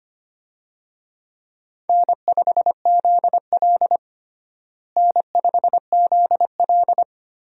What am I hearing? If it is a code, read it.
Morse code, heard as a keyed tone, about 25 wpm, 700 Hz. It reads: N5ZL N5ZL
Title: USA Call Signs - 4 Characters - Repeated 2x - 25wpm